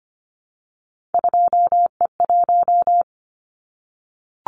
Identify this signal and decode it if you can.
Morse code, heard as a keyed tone, about 25 wpm, 700 Hz. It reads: 2E1
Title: Callsign Prefixes for Top 25 DXCC - 25wpm